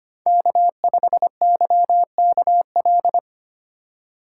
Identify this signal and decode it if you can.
Morse code, heard as a keyed tone, about 25 wpm, 700 Hz. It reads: K5YKL
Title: USA Call Signs - 5 Characters - 4x Character Spacing - 25wpm